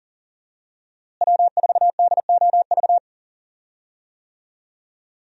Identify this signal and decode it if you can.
Morse code, heard as a keyed tone, about 40 wpm, 700 Hz. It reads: W4DOV
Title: USA Call Signs - 5 Characters - Repeated 3x - 40wpm